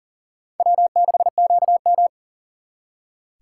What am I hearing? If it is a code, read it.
Morse code, heard as a keyed tone, about 52 wpm, 700 Hz. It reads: W6QM